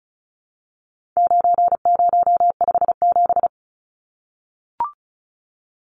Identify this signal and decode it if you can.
Morse code, heard as a keyed tone, about 34 wpm, 700 Hz. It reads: 9057
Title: Numbers - 4-Digit 35wpm